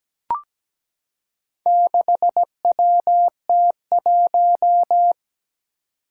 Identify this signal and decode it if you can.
Morse code, heard as a keyed tone, about 17 wpm, 700 Hz. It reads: 6WT1
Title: Four Challenging Letter-Number Combinations 17wpm